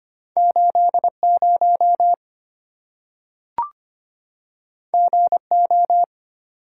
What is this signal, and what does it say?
Morse code, heard as a keyed tone, about 25 wpm, 700 Hz. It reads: 80 GO
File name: Two Letter-Number Combinations - 25wpm